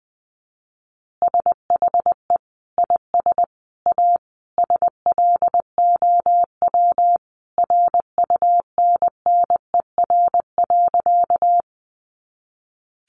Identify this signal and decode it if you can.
Morse code, heard as a keyed tone, about 20 wpm, 700 Hz. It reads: SHE IS A SLOW RUNNER.
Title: Additional Easy Phrases 20wpm